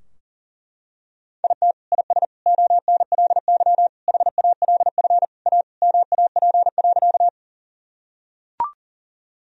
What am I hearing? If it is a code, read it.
Morse code, heard as a keyed tone, about 40 wpm, 700 Hz. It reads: IT IS ONLY HALF A MAP.
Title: Sentences from Top 1000 Words - Review Part 2 - 40wpm